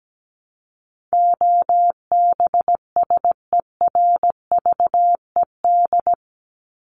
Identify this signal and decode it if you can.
Morse code, heard as a keyed tone, about 17 wpm, 700 Hz. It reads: OBSERVED
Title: ED Suffix - Top 100 Words with Base Words 17wpm